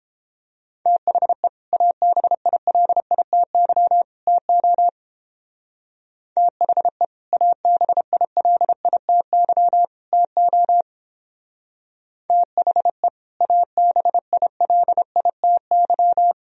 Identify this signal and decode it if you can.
Morse code, heard as a keyed tone, about 29 wpm, 700 Hz. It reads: THE ABILITY TO THE ABILITY TO THE ABILITY